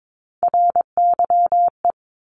Morse code, heard as a keyed tone, about 22 wpm, 700 Hz. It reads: RYE
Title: Multi-letter Phonemes Repeated 3x 22wpm